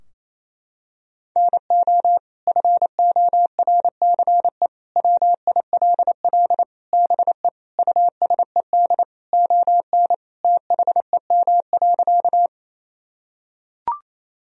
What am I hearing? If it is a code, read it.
Morse code, heard as a keyed tone, about 28 wpm, 700 Hz. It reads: NO FORCE WILL BE USED ON THEM.
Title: Sentences from Top 1000 Words - Review Part 3 - 28wpm